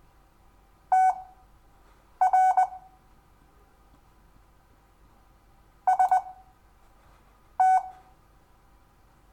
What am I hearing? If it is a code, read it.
Morse code, heard as a keyed tone, about 20 wpm, spaced as for 6 wpm, 1500 Hz. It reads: TR ST
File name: cw / Lesson5 LO